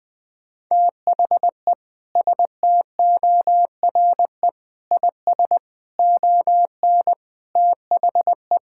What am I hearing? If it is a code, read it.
Morse code, heard as a keyed tone, about 20 wpm, 700 Hz. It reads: THE STORE IS ON THE
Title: Sentences with 5-Word Phrases - Spoken Beginning - 20wpm